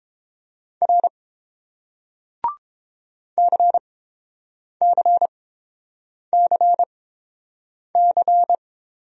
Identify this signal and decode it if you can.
Morse code, heard as a keyed tone, about 28 wpm, spaced as for 7 wpm, 700 Hz. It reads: R CCCC